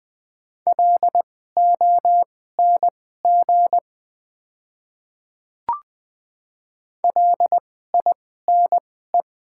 Morse code, heard as a keyed tone, about 20 wpm, spaced as for 14 wpm, 700 Hz. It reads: LONG LINE
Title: 2x Character Spacing - Top 200 Words - 20wpm